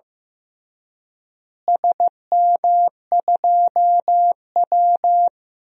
Morse code, heard as a keyed tone, about 15 wpm, 700 Hz. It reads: SM2W
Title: DXCC-100 Call Signs - 4 Characters - Repeated 3x - 15wpm